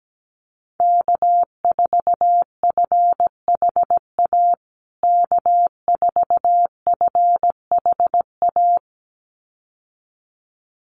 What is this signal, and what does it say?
Morse code, heard as a keyed tone, about 17 wpm, 700 Hz. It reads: K4FHA K4FHA
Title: USA Call Signs Repeated 2x 17wpm